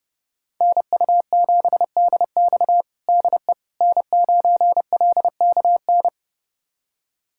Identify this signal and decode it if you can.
Morse code, heard as a keyed tone, about 30 wpm, 700 Hz. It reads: NU7DX DE N9LKN
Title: USA Callsign DE Callsign - 5 Characters - 30wpm